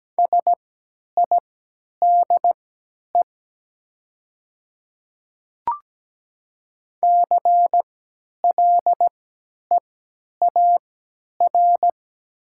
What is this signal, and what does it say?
Morse code, heard as a keyed tone, about 17 wpm, spaced as for 10 wpm, 690 Hz. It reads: SIDE CLEAR
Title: 3x Character Spacing - Top 500 Words - 17wpm